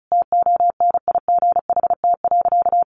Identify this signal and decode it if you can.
Morse code, heard as a keyed tone, about 34 wpm, 700 Hz. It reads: TONIGHT.
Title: Sentences from Top 600 Words 35wpm